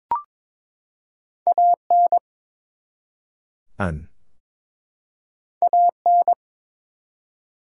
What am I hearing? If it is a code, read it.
Morse code, heard as a keyed tone, about 22 wpm, 700 Hz. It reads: AN AN
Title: Two Letter Words - 22wpm